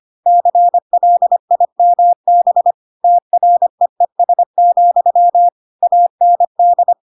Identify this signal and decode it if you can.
Morse code, heard as a keyed tone, about 25 wpm, 700 Hz. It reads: CLIMB TREES, AND
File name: Pinocchio / Pinocchio ch4 25 25 ik1hge